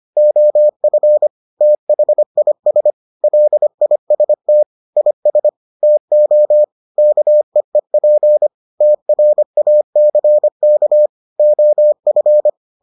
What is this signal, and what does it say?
Morse code, heard as a keyed tone, about 25 wpm, 600 Hz. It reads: OF THIS LIST IS TO KEEP TRACK OF